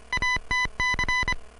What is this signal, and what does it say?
Morse code, heard as a keyed tone, about 25 wpm, 2000 Hz. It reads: ATC